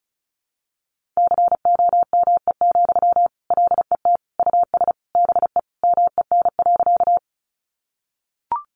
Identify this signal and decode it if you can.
Morse code, heard as a keyed tone, about 34 wpm, 700 Hz. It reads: COME, LET US BE MEN.
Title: Sentences from Top 300 Words 35wpm